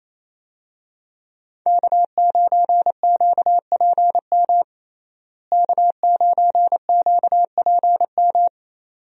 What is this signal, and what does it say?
Morse code, heard as a keyed tone, about 28 wpm, 700 Hz. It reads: K9QPM K9QPM
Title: USA Call Signs - 5 Characters - Repeated 2x - 28wpm